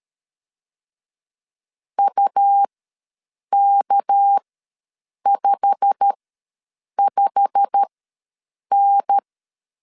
Morse code, heard as a keyed tone, about 13 wpm, spaced as for 7 wpm, 800 Hz. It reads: UK55N
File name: LEZIONI / 8 YJ45 / 8 YJ45 ws 12wpm 3146